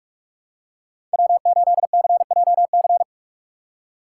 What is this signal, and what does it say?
Morse code, heard as a keyed tone, about 46 wpm, 700 Hz. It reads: W8CJC
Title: USA Call Signs - 5 Characters - 3x Character Spacing - 45wpm